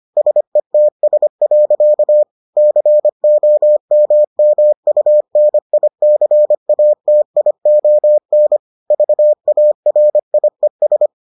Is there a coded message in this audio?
Yes: Morse code from 0.2 to 11.1 s, heard as a keyed tone, about 25 wpm, 600 Hz. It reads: SETS. COMMUNICATION VARIES